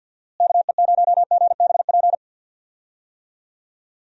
Morse code, heard as a keyed tone, about 51 wpm, 700 Hz. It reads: KE0GBP
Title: USA Call Signs - 6 Characters - Repeated 3x - 50wpm